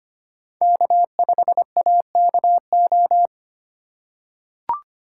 Morse code, heard as a keyed tone, about 25 wpm, 700 Hz. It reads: K5AKO
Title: USA Call Signs - 5 Characters - 3x Character Spacing - 25wpm